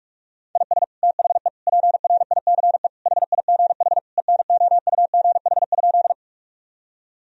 Morse code, heard as a keyed tone, about 46 wpm, 700 Hz. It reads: IS THE PRICE HIGH ENOUGH?
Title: Sentences from Top 1000 Words - Review Part 4 - 45wpm